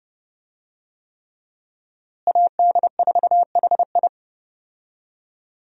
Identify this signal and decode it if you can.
Morse code, heard as a keyed tone, about 30 wpm, 700 Hz. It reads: AD4HI
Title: USA Call Signs - 5 Characters - Repeated 3x - 30wpm